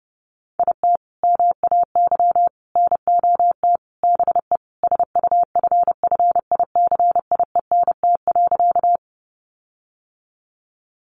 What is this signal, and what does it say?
Morse code, heard as a keyed tone, about 30 wpm, 700 Hz. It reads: IT MAY NOT BE SUFFICIENT.
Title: Sentences from Top 1400 Words 30wpm